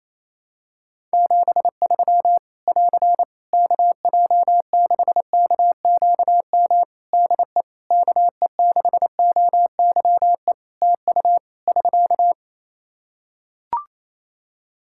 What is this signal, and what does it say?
Morse code, heard as a keyed tone, about 27 wpm, 700 Hz. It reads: 73 <AR> KJ6KQM DE KE6OYE TU <SK>
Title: Practice QSO - Exchange 4 - Ending - 28wpm